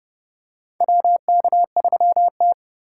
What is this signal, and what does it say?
Morse code, heard as a keyed tone, about 29 wpm, 700 Hz. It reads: WK3T
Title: USA Call Signs - 4 Characters - 3x Character Spacing - 30wpm